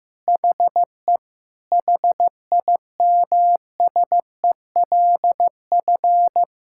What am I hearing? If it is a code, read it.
Morse code, heard as a keyed tone, about 15 wpm, 700 Hz. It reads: HE HIMSELF
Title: Sentences from Top 1600 Words 15wpm